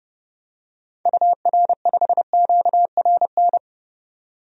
Morse code, heard as a keyed tone, about 30 wpm, 700 Hz. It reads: UR5QRN
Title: DXCC-100 Call Signs - 6 Characters - 30wpm